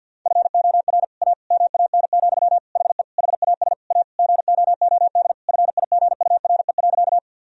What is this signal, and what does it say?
Morse code, heard as a keyed tone, about 50 wpm, 700 Hz. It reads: FOR A MAN, HE HAS A GOOD FIGURE.
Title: Sentences from Top 1000 Words - Review Part 2 - 50wpm